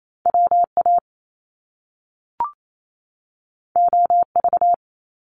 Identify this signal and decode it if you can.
Morse code, heard as a keyed tone, about 28 wpm, 700 Hz. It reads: W A O V
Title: Top Two Letter Combinations in Words 28wpm